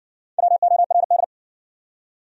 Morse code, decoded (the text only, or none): WORD